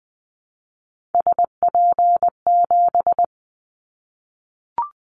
Morse code, heard as a keyed tone, about 20 wpm, 700 Hz. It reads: SP7
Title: Callsign Prefixes for Top 25 DXCC - Repeated 3x - 20wpm